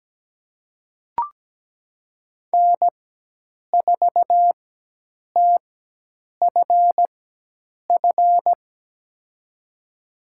Morse code, heard as a keyed tone, about 17 wpm, spaced as for 8 wpm, 700 Hz. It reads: N4TFF
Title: USA Call Signs - 5 Characters - 4x Character Spacing - 17wpm